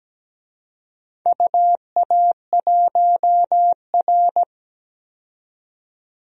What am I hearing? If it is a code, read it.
Morse code, heard as a keyed tone, about 17 wpm, 700 Hz. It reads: UA1R
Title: DXCC-100 Call Signs - 4 Characters - 17wpm